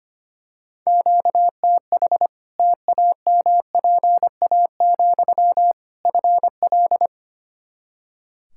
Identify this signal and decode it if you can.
Morse code, heard as a keyed tone, about 25 wpm, 700 Hz. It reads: QTH TAMPA, FL
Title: Practice QSO - Exchange 1 - Beginning - 25wpm